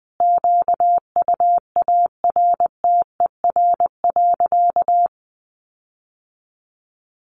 Morse code, heard as a keyed tone, about 20 wpm, 700 Hz. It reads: QUARTER.
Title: Sentences from Top 1500 Words 20wpm